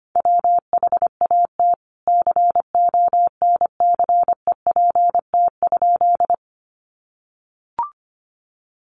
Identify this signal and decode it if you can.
Morse code, heard as a keyed tone, about 25 wpm, 700 Hz. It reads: WHAT CONCEPT?
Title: Sentences from Top 1100 Words 25wpm